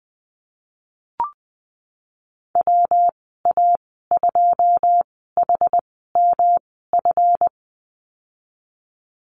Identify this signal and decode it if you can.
Morse code, heard as a keyed tone, about 20 wpm, spaced as for 15 wpm, 700 Hz. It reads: WA2HMF
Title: 154 Lesson 25g - Letter p - Call Signs 20wpm